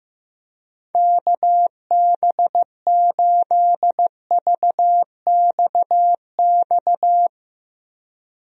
Morse code, heard as a keyed tone, about 15 wpm, 700 Hz. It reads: KB8VXX
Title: USA Call Signs - 6 Characters - Repeated 2x - 15wpm